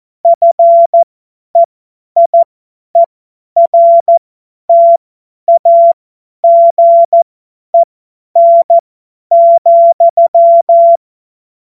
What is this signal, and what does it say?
Morse code, heard as a keyed tone, about 14 wpm, spaced as for 10 wpm, 680 Hz. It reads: FEIERTAGEN,
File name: Witz-12-August-2025-50-F70